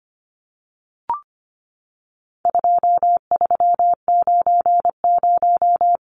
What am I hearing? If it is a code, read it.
Morse code, heard as a keyed tone, about 25 wpm, 700 Hz. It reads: 2390
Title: Numbers - 4-Digit 25wpm